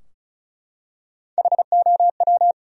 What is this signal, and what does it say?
Morse code, heard as a keyed tone, about 34 wpm, 700 Hz. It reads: HOW
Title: Top 100 Words Repeated 35wpm